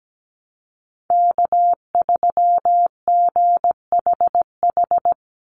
Morse code, heard as a keyed tone, about 17 wpm, 700 Hz. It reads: K3GHH
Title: Call Signs Repeated 3x - 17wpm